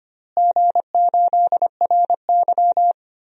Morse code, heard as a keyed tone, about 25 wpm, 700 Hz. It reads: G8RY
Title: DXCC-100 Call Signs - 4 Characters - Repeated 2x - 25wpm